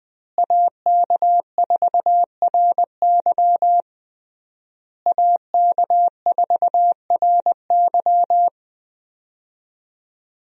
Morse code, heard as a keyed tone, about 20 wpm, 700 Hz. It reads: AK4RY AK4RY
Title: USA Call Signs - 5 Characters - Repeated 3x - 20wpm